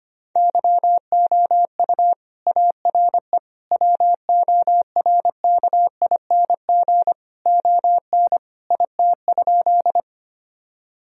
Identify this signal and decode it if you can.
Morse code, heard as a keyed tone, about 25 wpm, 700 Hz. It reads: YOU ARE WORKING ON IT?
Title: Sentences from Top 1000 Words - Review Part 4 - 25wpm